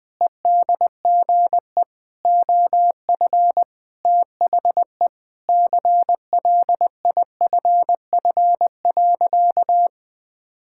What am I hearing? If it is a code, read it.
Morse code, heard as a keyed tone, about 21 wpm, 700 Hz. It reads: EDGE OF THE CLIFF.
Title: Sentences with 5-Word Phrases - 20wpm